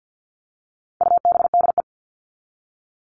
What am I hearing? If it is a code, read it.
Morse code, heard as a keyed tone, about 50 wpm, 700 Hz. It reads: UBDE
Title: All Cut Numbers - 4-Digit 50wpm